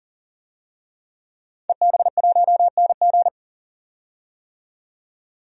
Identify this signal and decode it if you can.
Morse code, heard as a keyed tone, about 40 wpm, 700 Hz. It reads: EB1NG
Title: DXCC-100 Call Signs - 5 Characters - 40wpm